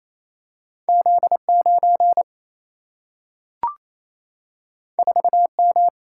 Morse code, heard as a keyed tone, about 28 wpm, 700 Hz. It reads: Z 9 4 M
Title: Single-Letter Single-Number 28wpm